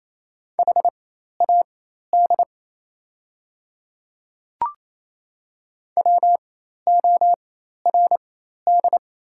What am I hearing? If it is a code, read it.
Morse code, heard as a keyed tone, about 28 wpm, spaced as for 13 wpm, 700 Hz. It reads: HAD WORD
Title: 4x Character Spacing - Top 100 Words - 28wpm